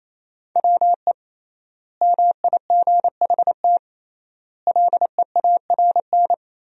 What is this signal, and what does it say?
Morse code, heard as a keyed tone, about 28 wpm, 700 Hz. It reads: WE MIGHT LEARN
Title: 3x Word-Spacing - Sentences from Top 300 Words - 28wpm